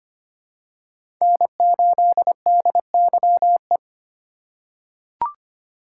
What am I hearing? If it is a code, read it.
Morse code, heard as a keyed tone, about 25 wpm, 700 Hz. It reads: N8DYE
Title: USA Call Signs - 5 Characters - 6x Character Spacing - 25wpm